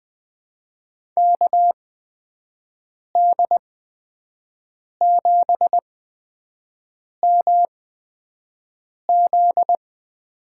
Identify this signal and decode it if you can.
Morse code, heard as a keyed tone, about 21 wpm, spaced as for 5 wpm, 700 Hz. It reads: KD7MZ
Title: USA Call Signs - 8x Character Spacing - 20wpm